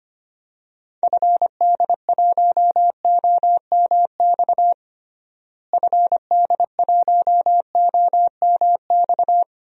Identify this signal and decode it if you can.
Morse code, heard as a keyed tone, about 25 wpm, 700 Hz. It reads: FD1OMX FD1OMX
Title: DXCC-100 Call Signs - 6 Characters - Repeated 2x - 25wpm